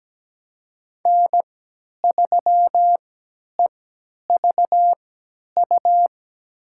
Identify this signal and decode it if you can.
Morse code, heard as a keyed tone, about 17 wpm, spaced as for 10 wpm, 700 Hz. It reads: N3EVU
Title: USA Call Signs - 5 Characters - 3x Character Spacing - 17wpm